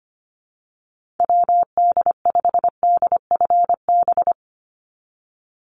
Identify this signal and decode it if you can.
Morse code, heard as a keyed tone, about 25 wpm, 690 Hz. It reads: WD5DFB
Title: USA Call Signs 25wpm